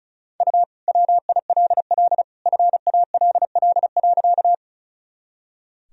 Morse code, heard as a keyed tone, about 34 wpm, 700 Hz. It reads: U WILL FALL.
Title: Sentences from Top 500 Words and Abbreviations 35wpm